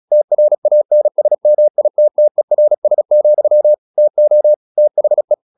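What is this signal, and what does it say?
Morse code, heard as a keyed tone, about 36 wpm, 600 Hz. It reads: TRANSMITTERS, TO THE
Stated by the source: Morse practice at 35 wpm from CW med Gnister, sending Ken Follett's novel Never in English.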